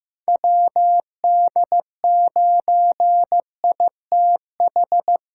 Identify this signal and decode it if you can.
Morse code, heard as a keyed tone, about 15 wpm, 700 Hz. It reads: WD9ITH
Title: USA Call Signs - 6 Characters - 6x Character Spacing - 15wpm